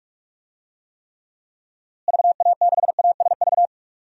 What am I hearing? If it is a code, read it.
Morse code, heard as a keyed tone, about 46 wpm, 700 Hz. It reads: VA6ASV
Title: DXCC-100 Call Signs - 6 Characters - Repeated 2x - 45wpm